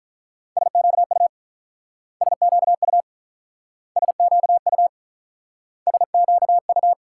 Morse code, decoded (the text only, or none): SQU SQU SQU SQU